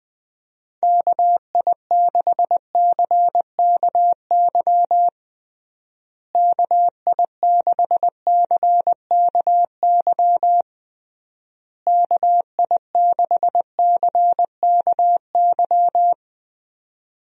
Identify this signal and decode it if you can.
Morse code, heard as a keyed tone, about 21 wpm, 700 Hz. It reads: KI6CKY KI6CKY KI6CKY
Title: USA Call Signs - 6 Characters - Repeated 3x - 20wpm